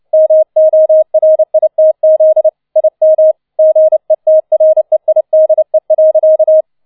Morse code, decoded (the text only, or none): MORITZ IM GETREIDE.